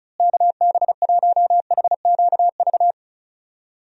Morse code, heard as a keyed tone, about 36 wpm, 700 Hz. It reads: KB1HQV
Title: USA Call Signs - 6 Characters - Repeated 2x - 35wpm